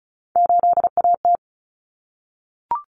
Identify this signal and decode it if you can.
Morse code, heard as a keyed tone, about 34 wpm, 700 Hz. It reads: 8AT
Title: Common Cut Numbers - 3-Digit 35wpm